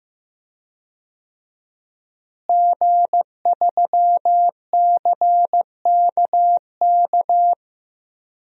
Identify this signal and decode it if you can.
Morse code, heard as a keyed tone, about 16 wpm, 700 Hz. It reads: G3CKK
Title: DXCC-100 Call Signs - 5 Characters - Repeated 2x - 15wpm